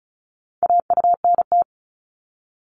Morse code, heard as a keyed tone, about 36 wpm, 700 Hz. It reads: AUNT
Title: Multi-letter Phonemes 35wpm